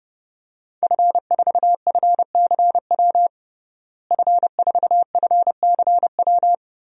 Morse code, heard as a keyed tone, about 30 wpm, 700 Hz. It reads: F4FCW F4FCW
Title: DXCC-100 Call Signs - 5 Characters - Repeated 3x - 30wpm